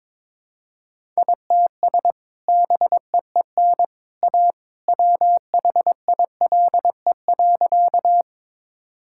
Morse code, heard as a keyed tone, about 22 wpm, 700 Hz. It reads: ITS BEEN A WHILE.
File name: practice / 22wpm / Sentences with 2-Word Phrases - 22wpm